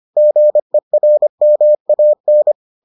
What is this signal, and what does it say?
Morse code, heard as a keyed tone, about 25 wpm, 600 Hz. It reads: GERMAN